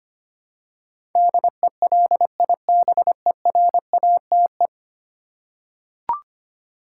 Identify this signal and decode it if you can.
Morse code, heard as a keyed tone, about 25 wpm, 700 Hz. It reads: DELIBERATE
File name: 10 Letter Words - First 3 Letters Given - 25wpm